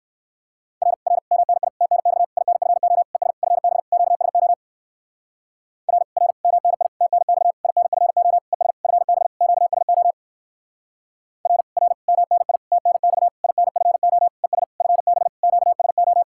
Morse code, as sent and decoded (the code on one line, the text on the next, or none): .-. / .-. / -- -. .. / - -. -..- / .. -. ..-. --- / . ... / ..-. -... / --.- ... --- / .-. / .-. / -- -. .. / - -. -..- / .. -. ..-. --- / . ... / ..-. -... / --.- ... --- / .-. / .-. / -- -. .. / - -. -..- / .. -. ..-. --- / . ... / ..-. -... / --.- ... ---
R R MNI TNX INFO ES FB QSO R R MNI TNX INFO ES FB QSO R R MNI TNX INFO ES FB QSO